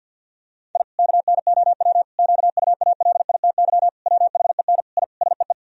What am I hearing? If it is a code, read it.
Morse code, heard as a keyed tone, about 50 wpm, 700 Hz. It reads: I KNOW QUALITY WHEN I SEE
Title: Sentences from Top 1000 Words - Review Part 4 - 50wpm